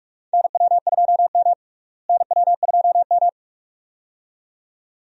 Morse code, heard as a keyed tone, about 44 wpm, 700 Hz. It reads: NW2M NW2M
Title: USA Call Signs - 4 Characters - Repeated 2x - 45wpm